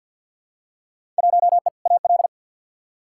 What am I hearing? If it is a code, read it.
Morse code, heard as a keyed tone, about 46 wpm, 700 Hz. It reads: 1E AL
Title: Field Day Exchanges - 50wpm